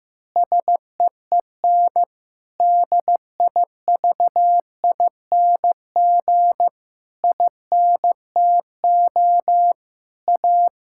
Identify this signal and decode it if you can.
Morse code, heard as a keyed tone, about 15 wpm, 700 Hz. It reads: SEEN DIVING INTO A
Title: Pangram Challenge v2 - 15wpm